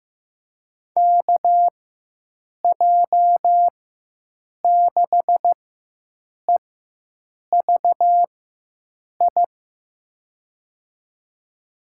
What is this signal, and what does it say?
Morse code, heard as a keyed tone, about 15 wpm, spaced as for 7 wpm, 700 Hz. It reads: KJ6EVI
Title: USA Call Signs - 6 Characters - 4x Character Spacing - 15wpm